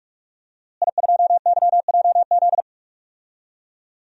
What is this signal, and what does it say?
Morse code, heard as a keyed tone, about 44 wpm, 700 Hz. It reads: I2YJZ
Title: DXCC-100 Call Signs - 5 Characters - Repeated 3x - 45wpm